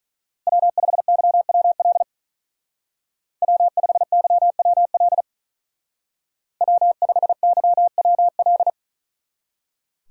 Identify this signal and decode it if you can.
Morse code, heard as a keyed tone, about 41 wpm, 700 Hz. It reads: W5YWL W5YWL W5YWL
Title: Speed Racing - USA 1x3 Call Signs 35wpm